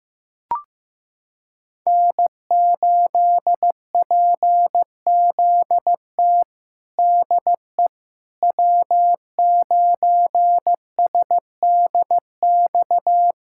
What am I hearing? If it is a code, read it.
Morse code, heard as a keyed tone, about 15 wpm, 700 Hz. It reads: N8PZT DE W9SDX
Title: USA Callsign DE Callsign - 5 Characters - 15wpm